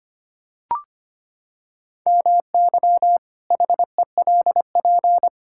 Morse code, heard as a keyed tone, about 25 wpm, 700 Hz. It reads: MY HELP
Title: Pangram Challenge v2 - 25wpm